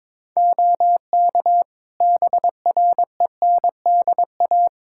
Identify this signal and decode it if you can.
Morse code, heard as a keyed tone, about 22 wpm, 700 Hz. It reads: OK BRENDA
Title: Practice QSO - Exchange 4 - Ending - 22wpm